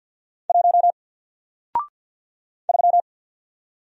Morse code, heard as a keyed tone, about 48 wpm, 700 Hz. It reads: 1 3
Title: Single Letter-Number Alt 50wpm